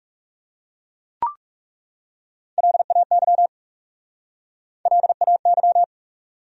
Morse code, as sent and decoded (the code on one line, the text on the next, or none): .-.. .- -.-- / .-.. .- -.--
LAY LAY